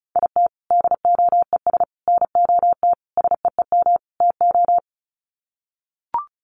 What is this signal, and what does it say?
Morse code, heard as a keyed tone, about 34 wpm, 700 Hz. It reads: IT DOES NOT SEEM TO
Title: Sets of 5 Words Encore2 35wpm